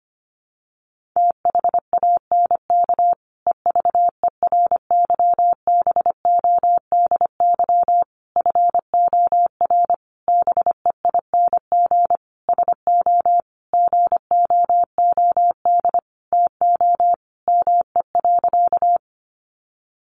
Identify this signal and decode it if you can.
Morse code, heard as a keyed tone, about 25 wpm, 700 Hz. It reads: THANK EVERYBODY FOR BEING SO GOOD TO ME.
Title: Sentences from Top 1400 Words 25wpm